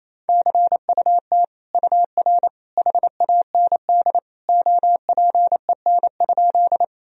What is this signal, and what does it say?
Morse code, heard as a keyed tone, about 28 wpm, 700 Hz. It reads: CUT UR HAND OPEN?
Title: Sentences from Top 300 Words and Abbreviations 28wpm